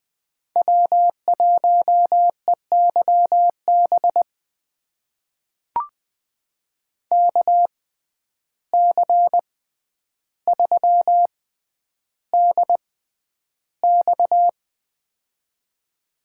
Morse code, heard as a keyed tone, about 20 wpm, spaced as for 10 wpm, 700 Hz. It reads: W1EYB KC3DX
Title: USA Call Signs - 5 Characters - 6x Character Spacing - 20wpm